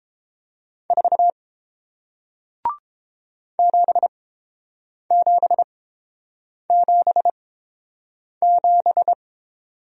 Morse code, heard as a keyed tone, about 30 wpm, spaced as for 7 wpm, 700 Hz. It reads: 4 7777